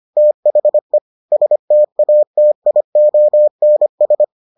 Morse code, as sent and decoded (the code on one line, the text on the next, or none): - .... / ... - .- - .. --- -. ...
THE STATIONS